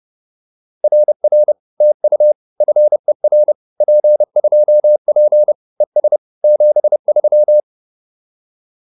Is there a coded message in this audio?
Yes: Morse code from 0.8 to 7.6 s, heard as a keyed tone, about 30 wpm, 600 Hz. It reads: RR TU FER P2P ES 73